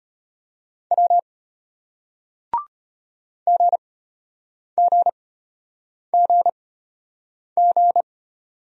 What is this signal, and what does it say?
Morse code, heard as a keyed tone, about 31 wpm, spaced as for 8 wpm, 700 Hz. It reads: W GGGG
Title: Speed-Racing Single Letters 25wpm